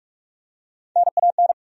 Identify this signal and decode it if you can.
Morse code, heard as a keyed tone, about 46 wpm, 700 Hz. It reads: NAN
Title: USA Common Names - Max 4 Letters - 45wpm